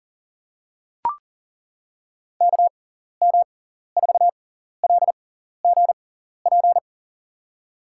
Morse code, heard as a keyed tone, about 40 wpm, spaced as for 14 wpm, 700 Hz. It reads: KM4LGP
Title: USA Call Signs - 6 Characters - 6x Character Spacing - 40wpm